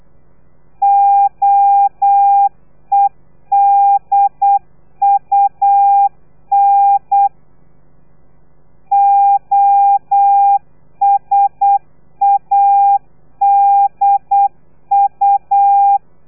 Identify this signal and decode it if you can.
Morse code, heard as a keyed tone, about 8 wpm, 800 Hz. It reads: OEDUN OSADU
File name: telegrafia / i2tfj / Lez3 35